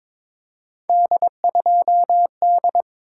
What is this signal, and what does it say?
Morse code, heard as a keyed tone, about 23 wpm, 700 Hz. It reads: D2D
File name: practice / 22wpm / Three Letter-Number Combinations - 22wpm